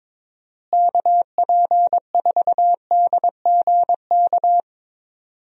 Morse code, heard as a keyed tone, about 22 wpm, 700 Hz. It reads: KP4DGK
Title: USA Call Signs - 6 Characters - Repeated 2x - 22wpm